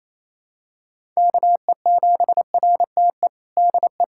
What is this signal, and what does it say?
Morse code, heard as a keyed tone, about 27 wpm, 700 Hz. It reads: KE7RTE DE
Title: USA Callsign DE Callsign - 6 Characters - 28wpm